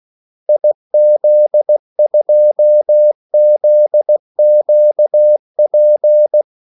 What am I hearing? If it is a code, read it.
Morse code, heard as a keyed tone, about 16 wpm, 600 Hz. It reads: IZ2ZQP
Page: Kaldesignaler 16 wpm